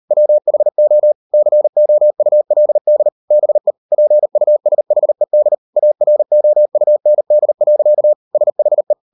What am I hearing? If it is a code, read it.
Morse code, heard as a keyed tone, about 38 wpm, 600 Hz. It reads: WHO COULD BE PUSHED AROUND. SHE